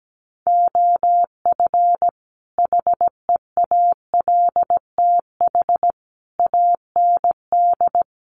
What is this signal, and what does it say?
Morse code, heard as a keyed tone, about 17 wpm, 700 Hz. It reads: OF HEALTH AND
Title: Sets of 5 Words 17wpm